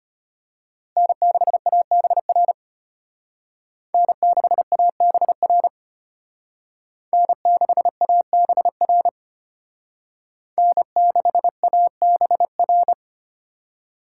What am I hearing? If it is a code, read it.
Morse code, heard as a keyed tone, about 29 wpm, 700 Hz. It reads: N6ABR N6ABR N6ABR N6ABR
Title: Speed Racing - USA 1x3 Call Signs 25wpm